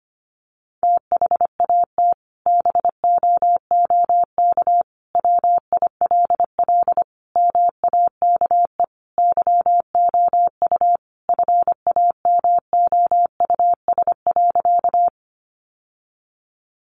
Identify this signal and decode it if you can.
Morse code, heard as a keyed tone, about 25 wpm, 700 Hz. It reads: THAT BOOK WILL MAKE YOU FAMOUS.
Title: Sentences from Top 1500 Words 25wpm